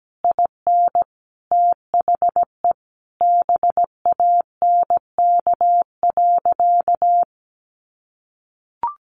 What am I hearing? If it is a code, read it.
Morse code, heard as a keyed tone, about 17 wpm, 700 Hz. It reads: IN THE BANK.
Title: Sentences from Top 1200 Words 17wpm